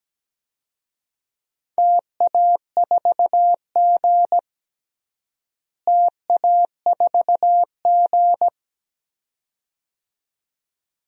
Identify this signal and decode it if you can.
Morse code, heard as a keyed tone, about 17 wpm, 700 Hz. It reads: TA4G TA4G
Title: DXCC-100 Call Signs - 4 Characters - Repeated 2x - 17wpm